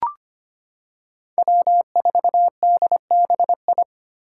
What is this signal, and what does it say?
Morse code, heard as a keyed tone, about 25 wpm, 700 Hz. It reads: W4DBI
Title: Five Challenging Letter-Number Combinations 25wpm